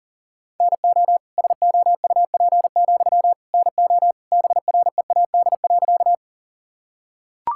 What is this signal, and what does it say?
Morse code, heard as a keyed tone, about 40 wpm, 700 Hz. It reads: NO SOUP, NO BREAD.
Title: Sentences from Top 1600 Words 40wpm